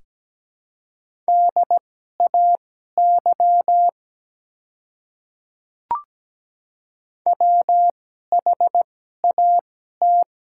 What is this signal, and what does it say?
Morse code, heard as a keyed tone, about 17 wpm, spaced as for 12 wpm, 690 Hz. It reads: DAY WHAT
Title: 2x Character Spacing - Top 100 Words - 17wpm